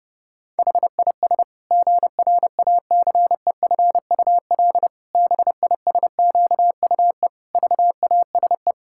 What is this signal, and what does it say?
Morse code, heard as a keyed tone, about 30 wpm, 700 Hz. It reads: HIS GRACEFUL BISQUE VASE
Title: Pangram Challenge v2 - 30wpm